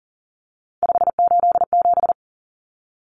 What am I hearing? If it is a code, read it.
Morse code, heard as a keyed tone, about 40 wpm, 700 Hz. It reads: 587
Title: Numbers - 3-Digit 40wpm